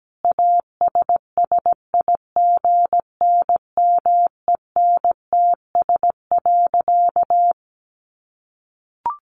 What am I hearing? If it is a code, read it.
Morse code, heard as a keyed tone, about 17 wpm, 700 Hz. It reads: ASSIGNMENTS.
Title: Easy Phrases 17wpm